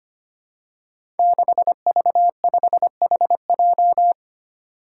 Morse code, heard as a keyed tone, about 25 wpm, 700 Hz. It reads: 6V5HJ
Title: Five Challenging Letter-Number Combinations 25wpm